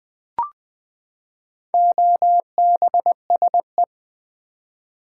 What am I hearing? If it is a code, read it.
Morse code, heard as a keyed tone, about 19 wpm, 700 Hz. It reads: OBSE
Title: Build A Word - 4 Letters At A Time - 11-Letter Words - 20wpm